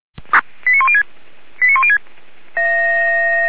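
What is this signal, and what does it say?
Five-tone selective call (ZVEI), heard as quick runs of five tones, tones of about 70 ms. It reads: 89106 89106
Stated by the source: ZVEI five-tone selective call received by SDR.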